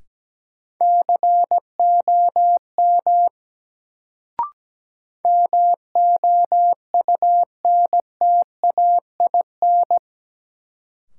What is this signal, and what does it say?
Morse code, heard as a keyed tone, about 17 wpm, 700 Hz. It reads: COM MOUNTAIN
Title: Sentences from Top 1300 Words - New Word Drill Alt - 17wpm